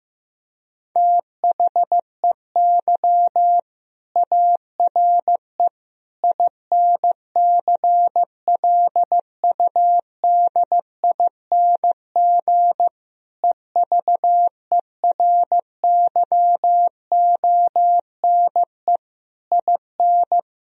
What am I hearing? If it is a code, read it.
Morse code, heard as a keyed tone, about 15 wpm, 700 Hz. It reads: THEY ARE INCLUDING EVERYONE IN
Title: Sentences from Top 1000 Words - Review Part 4 - 15wpm